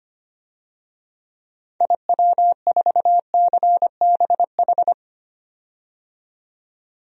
Morse code, heard as a keyed tone, about 25 wpm, 700 Hz. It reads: IW4CBH
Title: DXCC-100 Call Signs - 6 Characters - 25wpm